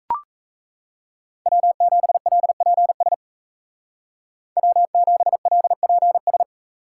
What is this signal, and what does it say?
Morse code, heard as a keyed tone, about 34 wpm, 700 Hz. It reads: W7LPS W7LPS
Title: Speed Racing - USA 1x3 Call Signs 28wpm